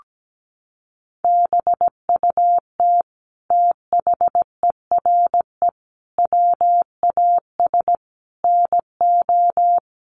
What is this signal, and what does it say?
Morse code, heard as a keyed tone, about 17 wpm, 700 Hz. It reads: BUT THERE WAS NO
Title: Sentences from Top 1400 Words 17wpm